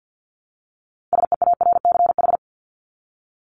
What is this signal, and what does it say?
Morse code, heard as a keyed tone, about 50 wpm, 700 Hz. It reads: SEARCH
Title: Top 501-750 Words 50wpm